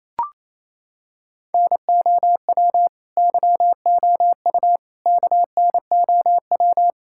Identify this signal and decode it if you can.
Morse code, heard as a keyed tone, about 28 wpm, 700 Hz. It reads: NOW YOU KNOW
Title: Sentences from Top 1000 Words - Review Part 6 - 28wpm